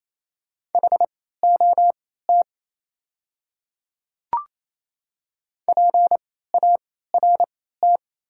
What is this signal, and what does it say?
Morse code, heard as a keyed tone, about 27 wpm, spaced as for 16 wpm, 700 Hz. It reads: HOT PART
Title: 3x Character Spacing - Top 200 Words - 28wpm